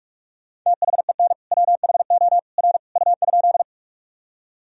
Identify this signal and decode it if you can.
Morse code, heard as a keyed tone, about 46 wpm, 700 Hz. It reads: THEN WHO R U?
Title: Sentences from Top 100 Words and Abbreviations 45wpm